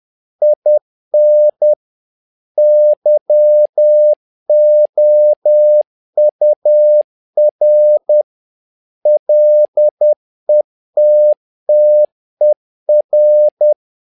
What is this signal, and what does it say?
Morse code, heard as a keyed tone, about 10 wpm, 600 Hz. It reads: IN YOUR LETTER
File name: learning center / tower trouble / tower trouble 10wpm